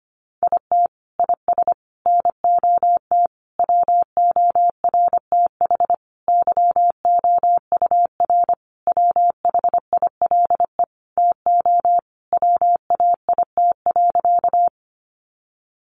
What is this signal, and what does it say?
Morse code, heard as a keyed tone, about 25 wpm, 700 Hz. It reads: IT IS NOT WORTH YOUR WHILE TO WAIT.
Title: Sentences from Top 800 Words 25wpm